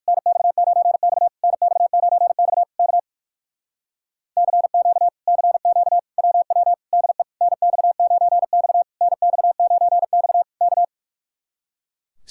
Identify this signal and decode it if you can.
Morse code, heard as a keyed tone, about 54 wpm, 700 Hz. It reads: NX9X NX9X K CQ CQ WW DE NX9X NX9X K
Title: Speed-Racing Common QSO Phrases - 45wpm